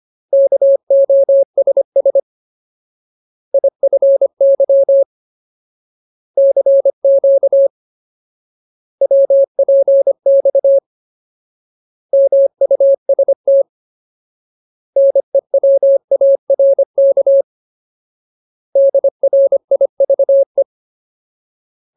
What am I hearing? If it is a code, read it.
Morse code, heard as a keyed tone, about 25 wpm, 550 Hz. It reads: KOSS IFY CQ WPX MUST NEWARK DRIVE
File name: Ultimate Word List / 25WPM Ultimate